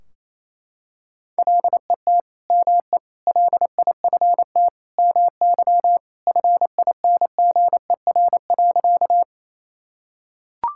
Morse code, heard as a keyed tone, about 28 wpm, 700 Hz. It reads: LET ME LIFT MY FINGER.
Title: Sentences from Top 1600 Words 28wpm